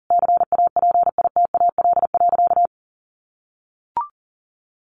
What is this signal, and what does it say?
Morse code, heard as a keyed tone, about 40 wpm, 700 Hz. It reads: CAPITAL.
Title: Sentences from Top 1500 Words 40wpm